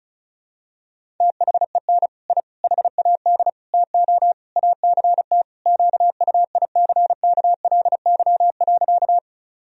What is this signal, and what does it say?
Morse code, heard as a keyed tone, about 34 wpm, 700 Hz. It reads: THEN I HAD TO ACT QUICKLY.
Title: Sentences from Top 1000 Words - Review Part 1 - 35wpm